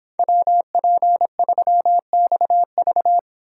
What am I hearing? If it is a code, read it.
Morse code, heard as a keyed tone, about 26 wpm, 700 Hz. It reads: WP3XV